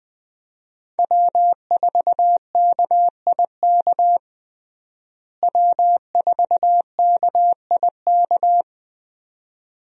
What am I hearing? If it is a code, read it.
Morse code, heard as a keyed tone, about 20 wpm, 700 Hz. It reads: W4KIK W4KIK
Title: USA Call Signs - 5 Characters - Repeated 2x - 20wpm